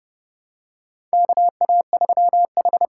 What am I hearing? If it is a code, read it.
Morse code, heard as a keyed tone, about 30 wpm, 700 Hz. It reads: KA3H
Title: USA Call Signs - 8x Character Spacing - 30wpm